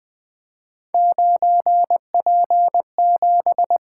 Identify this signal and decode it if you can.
Morse code, heard as a keyed tone, about 20 wpm, 700 Hz. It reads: 9P7
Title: Three Letter-Number Combinations - 20wpm